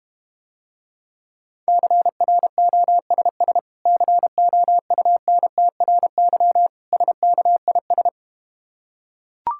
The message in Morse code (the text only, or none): CROSS COUNTRY SKIS